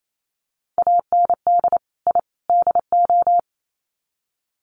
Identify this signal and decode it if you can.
Morse code, heard as a keyed tone, about 28 wpm, 700 Hz. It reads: AND I DO
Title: Sets of 3 Words 28wpm